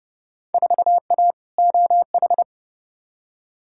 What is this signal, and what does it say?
Morse code, heard as a keyed tone, about 30 wpm, 700 Hz. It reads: 4A OH
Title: Field Day Exchanges - 30wpm